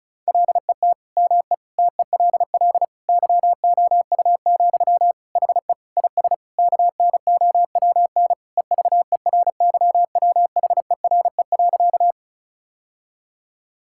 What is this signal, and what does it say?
Morse code, heard as a keyed tone, about 36 wpm, 700 Hz. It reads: LET ME TELL YOU, HE IS KNOWN EVERYWHERE.
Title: Sentences from Top 1600 Words 35wpm